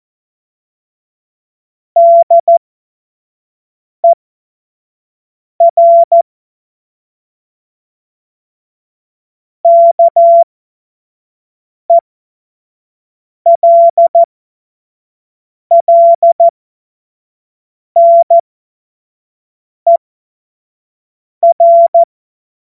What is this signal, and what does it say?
Morse code, heard as a keyed tone, about 14 wpm, spaced as for 5 wpm, 680 Hz. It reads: DER KELLNER
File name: Morsekurs / Witze / Witz 05. Oktober 2021, 25 F70